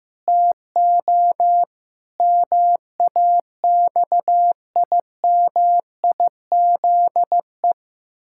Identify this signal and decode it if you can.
Morse code, heard as a keyed tone, about 15 wpm, 700 Hz. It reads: TO MAXIMIZE
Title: Pangram Challenge v2 - 15wpm